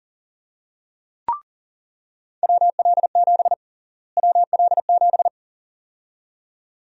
Morse code, heard as a keyed tone, about 40 wpm, 700 Hz. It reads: WL7 WL7
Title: USA Call Sign Prefixes - Repeated 2x - 40wpm